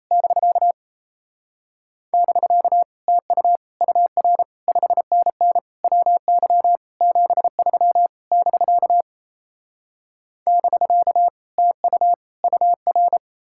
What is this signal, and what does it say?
Morse code, heard as a keyed tone, about 38 wpm, 700 Hz. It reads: <BK> <BK> TU UR 5NN WY 73 <BK> <BK> TU UR